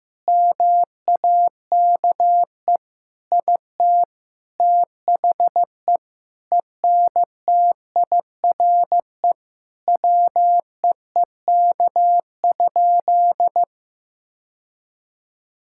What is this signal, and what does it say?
Morse code, heard as a keyed tone, about 16 wpm, 700 Hz. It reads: MAKE IT THE ENTIRE WEEK?
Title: Sentences from Top 1000 Words - Review Part 5 - 15wpm